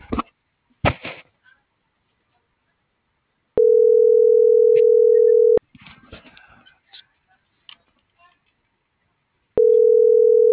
Ringback tone (North America), 440 + 480 Hz.